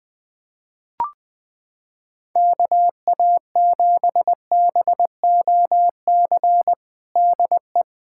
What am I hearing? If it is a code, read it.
Morse code, heard as a keyed tone, about 20 wpm, 700 Hz. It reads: KA7BOC DE
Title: USA Callsign DE Callsign - 6 Characters - 20wpm